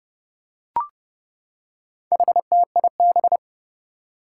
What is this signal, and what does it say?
Morse code, heard as a keyed tone, about 30 wpm, 700 Hz. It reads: HTIB
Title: Four Challenging Letter-Number Combinations 30wpm